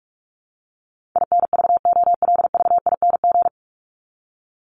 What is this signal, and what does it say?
Morse code, heard as a keyed tone, about 44 wpm, 700 Hz. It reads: INVOLVING
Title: ING Suffix - Top 100 Words with Base Words 45wpm